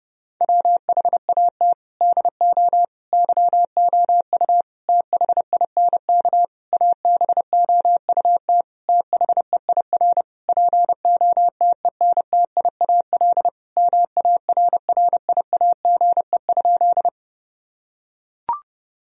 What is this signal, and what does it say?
Morse code, heard as a keyed tone, about 30 wpm, 700 Hz. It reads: WHAT DO YOU THINK ABOUT THEIR POTENTIAL MARRIAGE?
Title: Sentences from Top 1000 Words - Review Part 4 - 30wpm